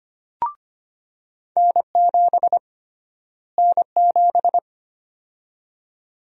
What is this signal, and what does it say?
Morse code, heard as a keyed tone, about 25 wpm, 700 Hz. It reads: N7 N7
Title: USA Call Sign Prefixes - Repeated 2x - 25wpm